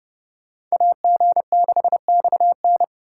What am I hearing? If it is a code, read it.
Morse code, heard as a keyed tone, about 37 wpm, 700 Hz. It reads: AG6XN
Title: Speed Racing - USA 2x2 Call Signs 30wpm